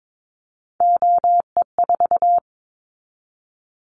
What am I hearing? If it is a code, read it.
Morse code, heard as a keyed tone, about 22 wpm, 700 Hz. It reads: OE4
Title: Callsign Prefixes for Top 50 DXCC - 22wpm